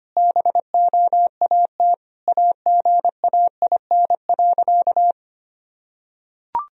Morse code, heard as a keyed tone, about 25 wpm, 700 Hz. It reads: BOAT AGAIN.
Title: Sentences from Top 1000 Words - Review Part 3 - 25wpm